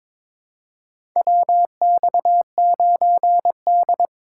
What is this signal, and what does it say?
Morse code, heard as a keyed tone, about 22 wpm, 700 Hz. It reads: WX9D
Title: USA Call Signs - 4 Characters - 2x Character Spacing - 22wpm